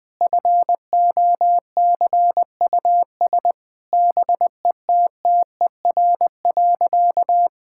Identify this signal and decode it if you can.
Morse code, heard as a keyed tone, about 21 wpm, 700 Hz. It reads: FOCUS BETTER.
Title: Difficult Sentences with Hints - 20wpm